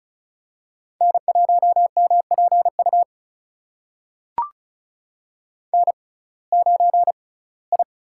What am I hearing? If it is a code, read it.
Morse code, heard as a keyed tone, about 36 wpm, spaced as for 22 wpm, 700 Hz. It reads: N1MPU N9I
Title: USA Call Signs - 5 Characters - 6x Character Spacing - 35wpm